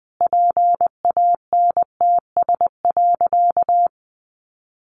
Morse code, heard as a keyed tone, about 20 wpm, 700 Hz. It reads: PANTS.
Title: Additional Easy Phrases 20wpm